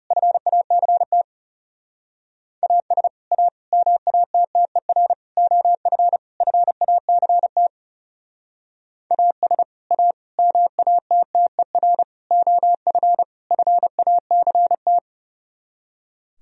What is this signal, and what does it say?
Morse code, heard as a keyed tone, about 36 wpm, 700 Hz. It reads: FACT AS A MATTER OF FACT AS A MATTER OF FACT
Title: Speed-Racing - Sets of 5 Words 30wpm